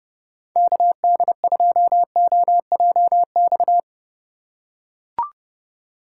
Morse code, heard as a keyed tone, about 30 wpm, 700 Hz. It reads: KD2OJX
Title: USA Call Signs - 6 Characters - 6x Character Spacing - 30wpm